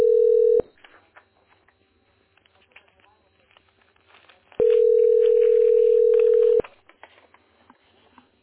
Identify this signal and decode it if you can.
Ringback tone (North America), 440 + 480 Hz, with speech.
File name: 5ca7c7df-4c29-4b7d-86b0